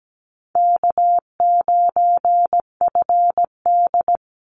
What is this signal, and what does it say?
Morse code, heard as a keyed tone, about 17 wpm, 700 Hz. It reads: K9FD
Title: Call Signs Repeated 3x - 17wpm